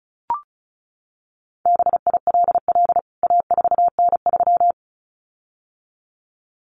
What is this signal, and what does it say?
Morse code, heard as a keyed tone, about 34 wpm, 700 Hz. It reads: BILL A4N3
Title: CWT Exchange 35wpm